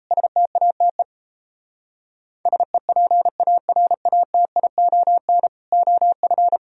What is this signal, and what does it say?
Morse code, heard as a keyed tone, about 38 wpm, 700 Hz. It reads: STATE SEPARATION OF